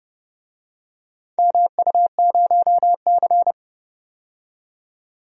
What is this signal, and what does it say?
Morse code, heard as a keyed tone, about 32 wpm, 700 Hz. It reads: MU0C
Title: DXCC-100 Call Signs - 4 Characters - 30wpm